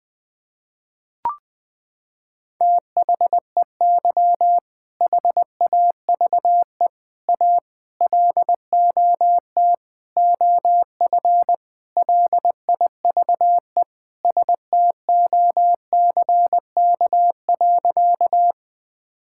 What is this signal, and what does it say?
Morse code, heard as a keyed tone, about 20 wpm, 700 Hz. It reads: THEY HAVE A LOT OF LIVE STOCK.
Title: Sentences from Top 1000 Words - Review Part 5 - 20wpm